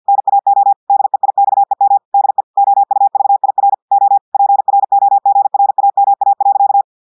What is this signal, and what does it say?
Morse code, heard as a keyed tone, about 50 wpm, 800 Hz. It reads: NAO DEIXEM DE OUVIR O PROGRAMA.
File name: portugais texte en clair 2 vitesse 50 mots minutes